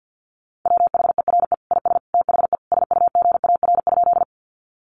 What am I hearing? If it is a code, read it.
Morse code, heard as a keyed tone, about 50 wpm, 700 Hz. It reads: WHERE IS THE SUGAR?
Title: Sentences from Top 1200 Words 50wpm